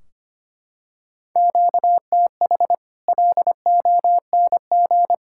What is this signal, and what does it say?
Morse code, heard as a keyed tone, about 25 wpm, 700 Hz. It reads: QTH LONG
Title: Practice QSO - Exchange 1 - Beginning - 25wpm